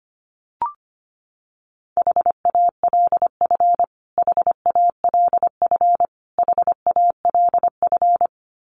Morse code, heard as a keyed tone, about 25 wpm, 700 Hz. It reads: HALF HALF HALF
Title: Top 500 Words Repeated 25wpm